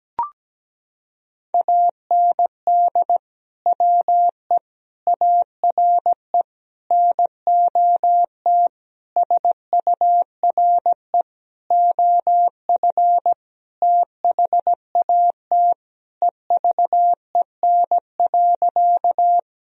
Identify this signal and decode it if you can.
Morse code, heard as a keyed tone, about 17 wpm, 700 Hz. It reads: AND WE ARE NOT SURE OF THAT EVEN.
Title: Sentences from Top 1000 Words - Review Part 1 - 17wpm